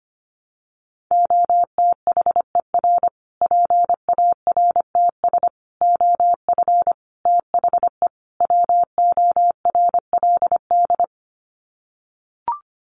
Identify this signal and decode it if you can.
Morse code, heard as a keyed tone, about 25 wpm, 700 Hz. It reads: OTHER PARTS OF THE WORLD
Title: Sets of 5 Words Encore 25wpm